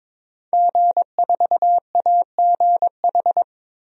Morse code, heard as a keyed tone, about 22 wpm, 700 Hz. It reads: G4AGH
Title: DXCC-100 Call Signs - 5 Characters - Repeated 3x - 22wpm